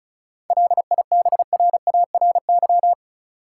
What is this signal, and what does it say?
Morse code, heard as a keyed tone, about 36 wpm, 700 Hz. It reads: LIBRARY
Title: Sentences from Top 600 Words - New Word Drill - 35wpm